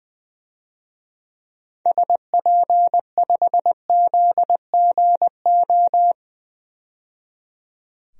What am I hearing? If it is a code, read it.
Morse code, heard as a keyed tone, about 20 wpm, 700 Hz. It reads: SP5ZGO
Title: DXCC-100 Call Signs - 6 Characters - 20wpm